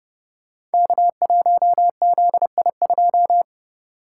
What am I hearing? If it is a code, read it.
Morse code, heard as a keyed tone, about 30 wpm, 700 Hz. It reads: K1ZI2
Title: Code Groups - 5 Letter:Numbers - 30wpm